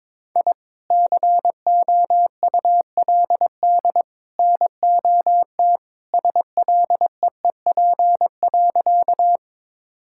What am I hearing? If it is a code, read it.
Morse code, heard as a keyed tone, about 22 wpm, 700 Hz. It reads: I COULD NOT SLEEP.
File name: practice / 22wpm / Sentences with 3-Word Phrases - 22wpm